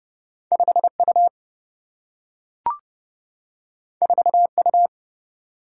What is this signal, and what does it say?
Morse code, heard as a keyed tone, about 30 wpm, 700 Hz. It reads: 5U 4U
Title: Single-Letter Single-Number 30wpm